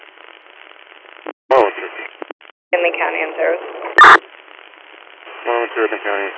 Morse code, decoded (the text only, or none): none